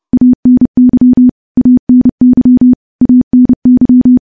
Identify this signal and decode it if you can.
Morse code, heard as a keyed tone, about 30 wpm, 250 Hz. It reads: ANY ANY ANY
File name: morse-training-30WPM